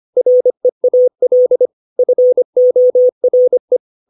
Morse code, heard as a keyed tone, about 25 wpm, 500 Hz. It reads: REAL FORE
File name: 632 words / 632 Words 25WPM